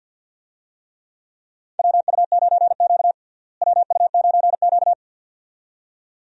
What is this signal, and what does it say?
Morse code, heard as a keyed tone, about 48 wpm, 700 Hz. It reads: WU9Q WU9Q
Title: USA Call Signs - 4 Characters - Repeated 2x - 50wpm